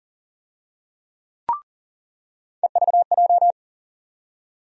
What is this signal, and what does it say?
Morse code, heard as a keyed tone, about 40 wpm, 700 Hz. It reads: EVJ
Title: USA Call Sign Postfixes - 3 Letters 40wpm